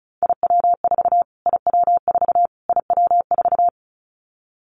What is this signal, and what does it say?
Morse code, heard as a keyed tone, about 34 wpm, 700 Hz. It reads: IW4 IW4 IW4
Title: Callsign Prefixes for Top 25 DXCC - Repeated 3x - 35wpm